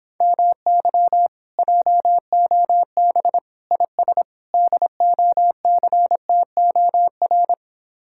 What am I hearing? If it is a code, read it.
Morse code, heard as a keyed tone, about 26 wpm, 700 Hz. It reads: MY JOB IS DOCTOR